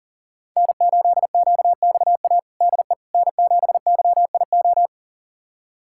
Morse code, heard as a keyed tone, about 37 wpm, 700 Hz. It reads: N8QXA DE N7YIO
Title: USA Callsign DE Callsign - 5 Characters - 40wpm